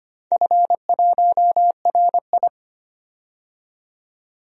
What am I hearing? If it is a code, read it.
Morse code, heard as a keyed tone, about 25 wpm, 700 Hz. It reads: F1RI
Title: DXCC-100 Call Signs - 4 Characters - 25wpm